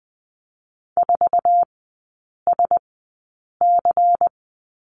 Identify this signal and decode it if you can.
Morse code, heard as a keyed tone, about 20 wpm, spaced as for 8 wpm, 700 Hz. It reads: 4SC
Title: 145 Lesson 24e - Letter y - Three Characters 20wpm